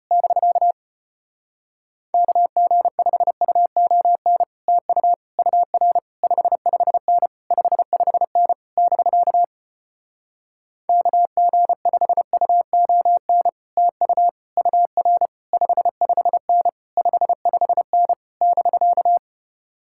Morse code, heard as a keyed tone, about 33 wpm, 700 Hz. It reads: <BK> KG5UON TU UR 55N 55N <BK> KG5UON TU UR 55N 55N <BK>